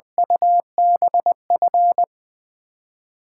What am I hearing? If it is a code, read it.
Morse code, heard as a keyed tone, about 21 wpm, 700 Hz. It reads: UBF
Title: USA Callsign DE Callsign - 6 Characters - 20wpm